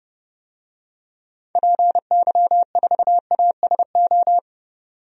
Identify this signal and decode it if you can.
Morse code, heard as a keyed tone, about 30 wpm, 700 Hz. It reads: PY4ASO
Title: DXCC-100 Call Signs - 6 Characters - 30wpm